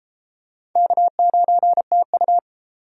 Morse code, heard as a keyed tone, about 28 wpm, 700 Hz. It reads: K9TU